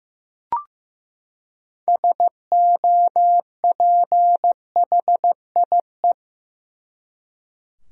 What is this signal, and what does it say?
Morse code, heard as a keyed tone, about 15 wpm, 700 Hz. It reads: SOPHIE
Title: USA Common Names - 15wpm